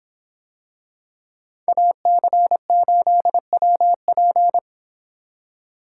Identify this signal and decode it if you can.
Morse code, heard as a keyed tone, about 25 wpm, 700 Hz. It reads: AC8WP